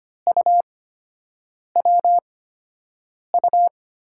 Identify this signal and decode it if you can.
Morse code, heard as a keyed tone, about 25 wpm, spaced as for 7 wpm, 700 Hz. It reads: UWU
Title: USA Call Signs - 5 Characters - 8x Character Spacing - 25wpm